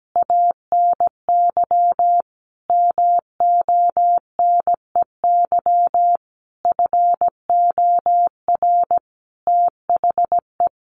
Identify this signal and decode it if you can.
Morse code, heard as a keyed tone, about 17 wpm, 700 Hz. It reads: ANY MONEY FOR THE
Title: Sentences from Top 1400 Words 17wpm